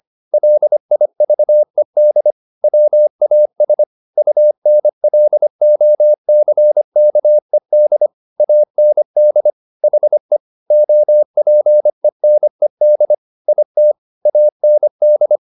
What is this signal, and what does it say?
Morse code, heard as a keyed tone, about 25 wpm, 600 Hz. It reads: LIVED WAS UNLOCKED AND HE OPENED IT AND
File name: the old man and the sea part 1 25wpm